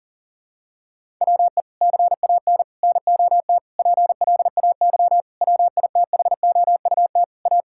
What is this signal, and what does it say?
Morse code, heard as a keyed tone, about 40 wpm, 700 Hz. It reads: WE CAN NOT PLAY WITHOUT A
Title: Sentences from Top 1000 Words - Review Part 6 - 40wpm